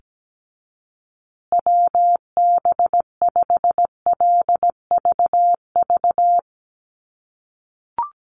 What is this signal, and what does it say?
Morse code, heard as a keyed tone, about 17 wpm, 700 Hz. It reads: WB5LVV
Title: USA Call Signs Repeated 2x 17wpm